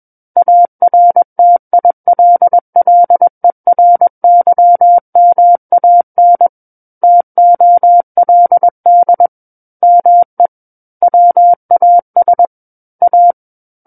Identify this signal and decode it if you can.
Morse code, heard as a keyed tone, about 21 wpm, 700 Hz. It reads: ARTILLERYMAN TOLD ME WAS A
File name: Book One / 12-Chapter 12 - 21 WPM